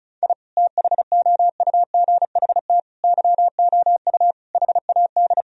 Morse code, heard as a keyed tone, about 36 wpm, 700 Hz. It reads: I THOUGHT YOU HAD
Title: Sentences from Top 1000 Words - Review Part 4 - 35wpm